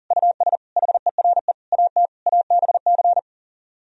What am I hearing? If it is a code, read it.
Morse code, heard as a keyed tone, about 53 wpm, spaced as for 46 wpm, 700 Hz. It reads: US HERE AT ABC
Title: Speed-Racing - Sets of 5 Words 40wpm